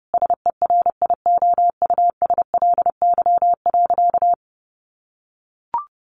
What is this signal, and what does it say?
Morse code, heard as a keyed tone, about 30 wpm, 700 Hz. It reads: SERIOUSLY.
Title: Sentences from Top 1200 Words 30wpm